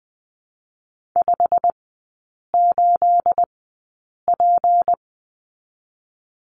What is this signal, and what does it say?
Morse code, heard as a keyed tone, about 20 wpm, spaced as for 8 wpm, 700 Hz. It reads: 58P
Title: 218 Lesson 35e - Number 0 - Three Characters 20wpm